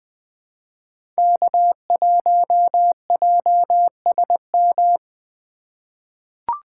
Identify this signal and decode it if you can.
Morse code, heard as a keyed tone, about 20 wpm, 700 Hz. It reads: K1JSM
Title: USA Call Signs - 3x Character Spacing - 20wpm